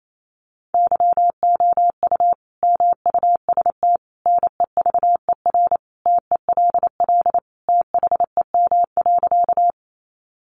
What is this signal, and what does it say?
Morse code, heard as a keyed tone, about 28 wpm, 700 Hz. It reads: YOU MUST NEVER TELL THEM.
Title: Sentences from Top 900 Words 28wpm